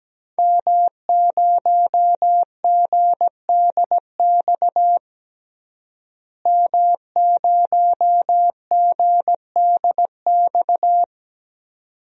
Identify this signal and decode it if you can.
Morse code, heard as a keyed tone, about 17 wpm, 700 Hz. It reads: M0GDX M0GDX
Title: DXCC-100 Call Signs - 5 Characters - Repeated 2x - 17wpm